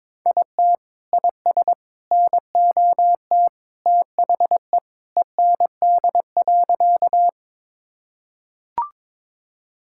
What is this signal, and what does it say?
Morse code, heard as a keyed tone, about 22 wpm, 700 Hz. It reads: IT IS NOT THE END.
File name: practice / 22wpm / Sentences with 3-Word Phrases - 22wpm